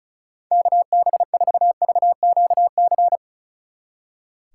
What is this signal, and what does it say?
Morse code, heard as a keyed tone, about 33 wpm, 700 Hz. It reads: KB4VQC
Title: Speed Racing - USA 2x3 Call Signs 35wpm